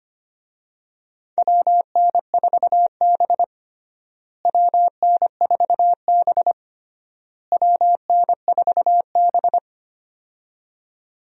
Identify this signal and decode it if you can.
Morse code, heard as a keyed tone, about 25 wpm, 700 Hz. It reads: WN4B WN4B WN4B
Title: USA Call Signs - 4 Characters - Repeated 3x - 25wpm